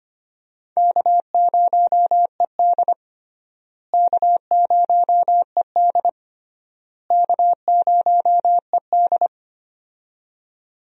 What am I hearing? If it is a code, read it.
Morse code, heard as a keyed tone, about 25 wpm, 700 Hz. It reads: K0ED K0ED K0ED
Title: USA Call Signs - 4 Characters - Repeated 3x - 25wpm